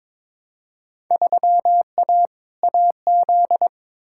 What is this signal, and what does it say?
Morse code, heard as a keyed tone, about 22 wpm, 700 Hz. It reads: 3A AZ
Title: Field Day Exchanges - 22wpm